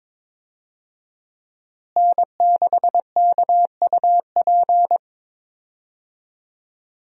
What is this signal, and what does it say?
Morse code, heard as a keyed tone, about 22 wpm, 700 Hz. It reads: N6KUP
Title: USA Call Signs - 5 Characters - Repeated 3x - 22wpm